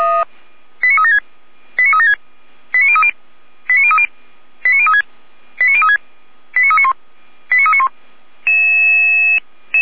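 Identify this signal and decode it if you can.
Five-tone selective call (ZVEI), heard as quick runs of five tones, tones of about 80 ms. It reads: 89376 89376 89030 89030 89036 89036 89391 89391
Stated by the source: ZVEI five-tone selective call 89050 received by SDR.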